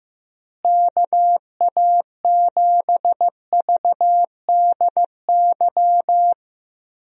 Morse code, heard as a keyed tone, about 15 wpm, 700 Hz. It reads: KA7VDY
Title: USA Call Signs - 6 Characters - Repeated 2x - 15wpm